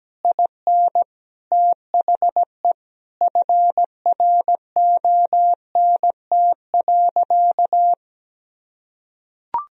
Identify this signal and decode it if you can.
Morse code, heard as a keyed tone, about 17 wpm, 700 Hz. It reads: IN THE FRONT.
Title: Sentences from Top 1000 Words - Review Part 2 - 17wpm